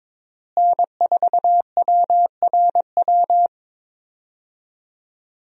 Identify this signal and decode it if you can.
Morse code, heard as a keyed tone, about 22 wpm, 700 Hz. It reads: N4WRW
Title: USA Call Signs - 5 Characters - Repeated 3x - 22wpm